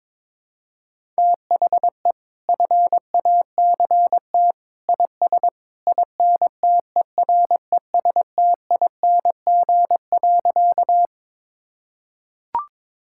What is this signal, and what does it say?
Morse code, heard as a keyed tone, about 22 wpm, 700 Hz. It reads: THE FACT IS INTERESTING.
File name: practice / 22wpm / Sentences with 2-Word Phrases - 22wpm